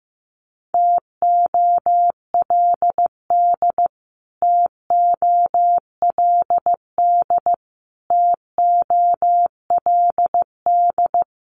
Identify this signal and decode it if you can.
Morse code, heard as a keyed tone, about 15 wpm, 700 Hz. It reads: TOLD TOLD TOLD
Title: Top 500 Words Repeated 15wpm